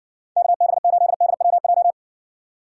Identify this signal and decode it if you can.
Morse code, heard as a keyed tone, about 60 wpm, 700 Hz. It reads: KB8DWJ